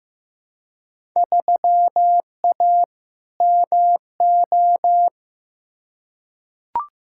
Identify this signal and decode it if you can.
Morse code, heard as a keyed tone, about 15 wpm, 700 Hz. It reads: 3A MO
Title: Field Day Exchanges - 15wpm